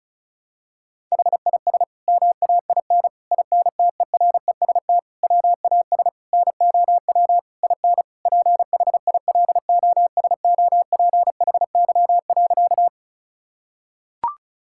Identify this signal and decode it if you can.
Morse code, heard as a keyed tone, about 34 wpm, 700 Hz. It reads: HIS MAIN INTEREST WAS NOW IN PHILOSOPHY.
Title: Sentences from Top 1000 Words - Review Part 6 - 35wpm